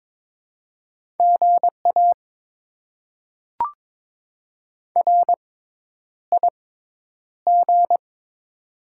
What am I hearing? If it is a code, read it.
Morse code, heard as a keyed tone, about 22 wpm, spaced as for 10 wpm, 700 Hz. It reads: GA RIG
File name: Top 20 QSO Elements - 6x Character Spacing - 22wpm